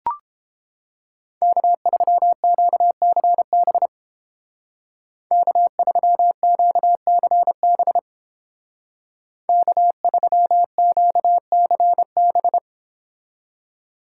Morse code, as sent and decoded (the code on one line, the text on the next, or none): -.- ...-- --.- -.-. -... / -.- ...-- --.- -.-. -... / -.- ...-- --.- -.-. -...
K3QCB K3QCB K3QCB